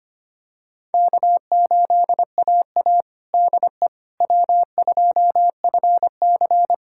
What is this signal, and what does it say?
Morse code, heard as a keyed tone, about 25 wpm, 700 Hz. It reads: K8AA DE W2FC
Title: USA Callsign DE Callsign - 4 Characters - 25wpm